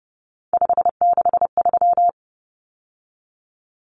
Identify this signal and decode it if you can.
Morse code, heard as a keyed tone, about 30 wpm, 700 Hz. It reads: 563
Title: Common Cut Numbers - 3-Digit 30wpm